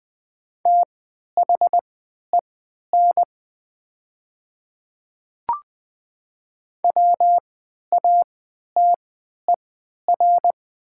Morse code, heard as a keyed tone, about 20 wpm, spaced as for 11 wpm, 700 Hz. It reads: THEN WATER
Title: 3x Character Spacing - Top 100 Words - 20wpm